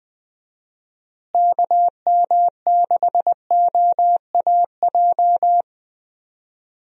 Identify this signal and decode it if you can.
Morse code, heard as a keyed tone, about 20 wpm, 700 Hz. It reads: KM6OAJ
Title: USA Call Signs - 6 Characters - Repeated 3x - 20wpm